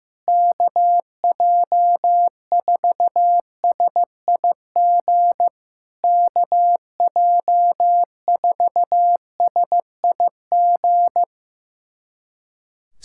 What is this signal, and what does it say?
Morse code, heard as a keyed tone, about 15 wpm, 700 Hz. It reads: KJ4SIG KJ4SIG
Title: SOTA QSO Practice 15wpm